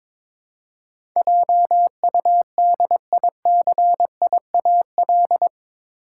Morse code, heard as a keyed tone, about 22 wpm, 700 Hz. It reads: JUDICIAL
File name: practice / 22wpm / Eight Letter Words - 22wpm